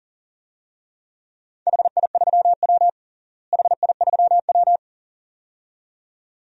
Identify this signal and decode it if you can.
Morse code, heard as a keyed tone, about 40 wpm, 700 Hz. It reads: HI3W HI3W
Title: DXCC-100 Call Signs - 4 Characters - Repeated 2x - 40wpm